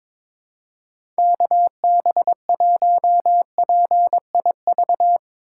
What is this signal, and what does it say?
Morse code, heard as a keyed tone, about 22 wpm, 700 Hz. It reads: KB1PIV
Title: USA Call Signs - 3x Character Spacing - 22wpm